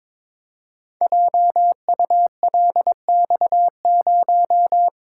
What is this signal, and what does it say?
Morse code, heard as a keyed tone, about 22 wpm, 700 Hz. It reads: JULX0
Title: Code Groups - 5 Letter:Numbers - 22wpm